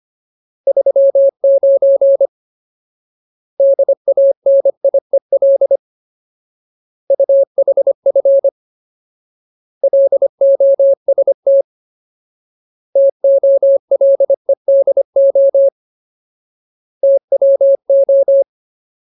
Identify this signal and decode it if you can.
Morse code, heard as a keyed tone, about 25 wpm, 550 Hz. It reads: 39 DANIEL UHF LOST TOLEDO TWO
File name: Ultimate Word List / 25WPM Ultimate